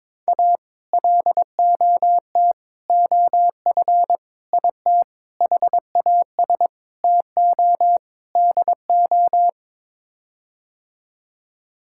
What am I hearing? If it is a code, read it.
Morse code, heard as a keyed tone, about 22 wpm, 700 Hz. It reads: A LOT OF IT HAS TO DO
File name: practice / 22wpm / Sets of 7 Words - 22wpm